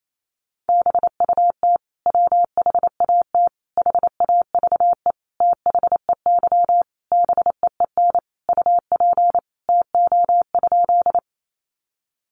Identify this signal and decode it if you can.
Morse code, heard as a keyed tone, about 28 wpm, 700 Hz. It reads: BUT WHAT HAVE THEY BEEN UP TO?
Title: Sentences from Top 100 Words 28wpm